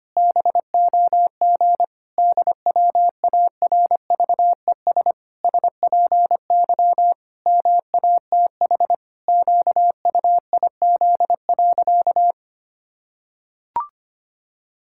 Morse code, heard as a keyed tone, about 25 wpm, 700 Hz. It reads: BOG DWARVES SPY MATH QUIZ.
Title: Pangram Challenge v2 - 25wpm